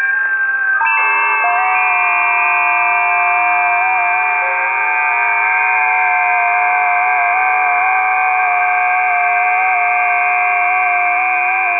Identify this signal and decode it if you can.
FT8.
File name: tests / 20m busy / test 17